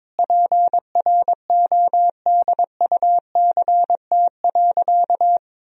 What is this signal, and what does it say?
Morse code, heard as a keyed tone, about 22 wpm, 700 Hz. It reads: PRODUCT.
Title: Sentences from Top 1000 Words - Review Part 2 - 22wpm